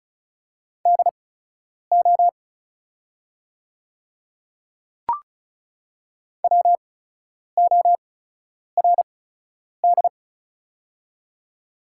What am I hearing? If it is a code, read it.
Morse code, heard as a keyed tone, about 36 wpm, spaced as for 10 wpm, 690 Hz. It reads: DO WORD
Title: 8x Character Spacing - Top 100 Words - 35wpm